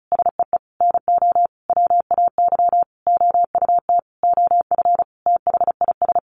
Morse code, heard as a keyed tone, about 34 wpm, 700 Hz. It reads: SEE NO WAY OUT OF THIS
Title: Sentences from Top 1200 Words 35wpm